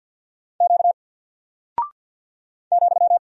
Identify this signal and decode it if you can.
Morse code, heard as a keyed tone, about 48 wpm, 700 Hz. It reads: Q ,
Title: Single Character (Letter:Number:Punctuation:Prosign) Alt 50wpm